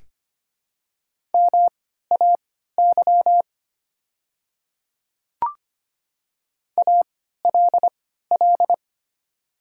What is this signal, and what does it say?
Morse code, heard as a keyed tone, about 25 wpm, spaced as for 14 wpm, 700 Hz. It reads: MAY ALL
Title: 3x Character Spacing - Top 100 Words - 25wpm